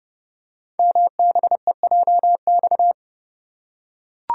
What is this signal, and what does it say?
Morse code, heard as a keyed tone, about 30 wpm, 690 Hz. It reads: MBEJX
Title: Code Groups - 5 Letters - 30wpm